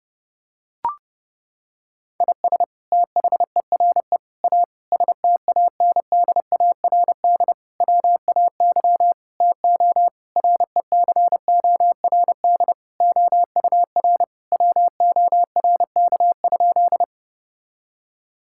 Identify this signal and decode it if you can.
Morse code, heard as a keyed tone, about 32 wpm, 700 Hz. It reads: IS THERE A STANDARD WAY TO RECORD OUR WORK?
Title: Sentences from Top 1000 Words - Review Part 6 - 30wpm